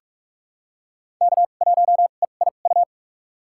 Morse code, heard as a keyed tone, about 45 wpm, spaced as for 32 wpm, 700 Hz. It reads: K1EIU
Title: USA Call Signs - 2x Character Spacing - 45wpm